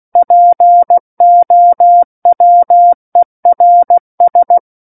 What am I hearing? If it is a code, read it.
Morse code, heard as a keyed tone, about 16 wpm, 700 Hz. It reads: POWERS
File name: Book One / 07-Chapter 7 - 16 WPM